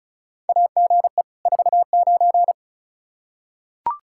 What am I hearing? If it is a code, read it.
Morse code, heard as a keyed tone, about 36 wpm, 700 Hz. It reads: AGE 49
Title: K1USN SST QSO Practice 35wpm